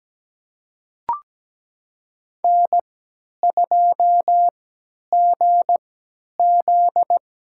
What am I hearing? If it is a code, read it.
Morse code, heard as a keyed tone, about 17 wpm, spaced as for 10 wpm, 700 Hz. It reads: N2GZ
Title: USA Call Signs - 4 Characters - 3x Character Spacing - 17wpm